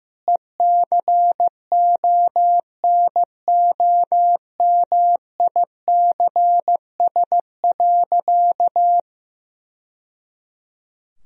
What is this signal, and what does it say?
Morse code, heard as a keyed tone, about 15 wpm, 700 Hz. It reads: ECONOMICS.
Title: Sentences from Top 1000 Words - Review Part 3 - 15wpm